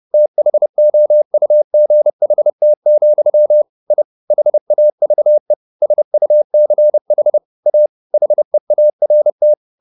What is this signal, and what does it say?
Morse code, heard as a keyed tone, about 30 wpm, 600 Hz. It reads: THOUGHT, I HAVE SUCH A HEART
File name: the old man and the sea / the old man and the sea 1 30wpm